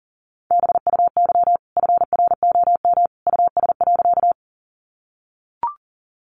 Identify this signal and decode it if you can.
Morse code, heard as a keyed tone, about 40 wpm, 700 Hz. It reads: BUY FROM US.
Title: Sentences from Top 700 Words 40wpm